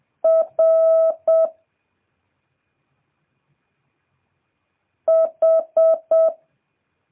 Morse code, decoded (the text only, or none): R H